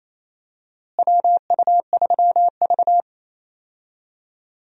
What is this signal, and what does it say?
Morse code, heard as a keyed tone, about 28 wpm, 700 Hz. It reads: WU3V
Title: USA Call Signs - 4 Characters - Repeated 3x - 28wpm